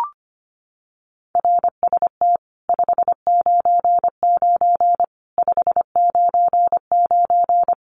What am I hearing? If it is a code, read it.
Morse code, heard as a keyed tone, about 25 wpm, 700 Hz. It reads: RST 599 599
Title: Common QSO Phrases 25wpm